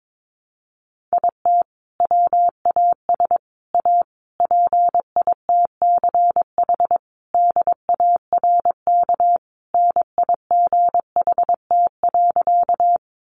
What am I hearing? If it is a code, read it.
Morse code, heard as a keyed tone, about 22 wpm, 700 Hz. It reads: IT WAS A PITCH DARK NIGHT.
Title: Sentences from Top 1300 Words 22wpm